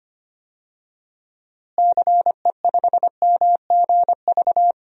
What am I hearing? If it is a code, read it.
Morse code, heard as a keyed tone, about 25 wpm, 700 Hz. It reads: CE5MGV
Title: DXCC-100 Call Signs - 6 Characters - Repeated 3x - 25wpm